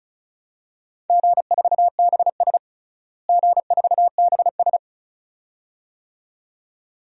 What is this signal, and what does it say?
Morse code, heard as a keyed tone, about 36 wpm, 700 Hz. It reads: G4BS G4BS
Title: DXCC-100 Call Signs - 4 Characters - Repeated 2x - 35wpm